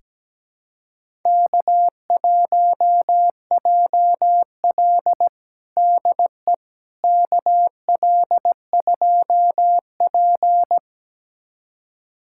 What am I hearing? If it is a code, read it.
Morse code, heard as a keyed tone, about 17 wpm, 700 Hz. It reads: K1JL DE KL2P
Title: USA Callsign DE Callsign - 4 Characters - 17wpm